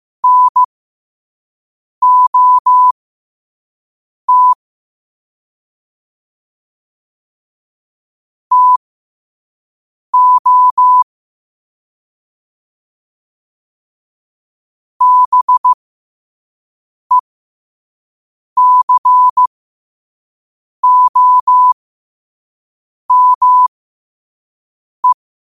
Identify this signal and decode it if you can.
Morse code, heard as a keyed tone, about 15 wpm, spaced as for 5 wpm, 1000 Hz. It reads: NOT TO BECOME
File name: qotdfiles / 5wpm